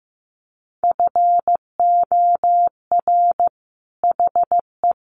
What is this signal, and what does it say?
Morse code, heard as a keyed tone, about 15 wpm, 700 Hz. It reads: FOR HE
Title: Sentences from Top 1400 Words 15wpm